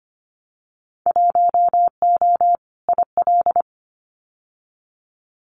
Morse code, heard as a keyed tone, about 25 wpm, 700 Hz. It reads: 1O IL
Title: Winter Field Day Exchanges 25wpm